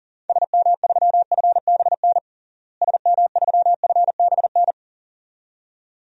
Morse code, heard as a keyed tone, about 40 wpm, 700 Hz. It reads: SM3FBN SM3FBN
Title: DXCC-100 Call Signs - 6 Characters - Repeated 2x - 40wpm